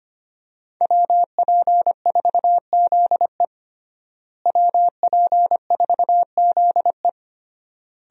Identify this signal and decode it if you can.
Morse code, heard as a keyed tone, about 25 wpm, 700 Hz. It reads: WP4ZE WP4ZE
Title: USA Call Signs - 5 Characters - Repeated 3x - 25wpm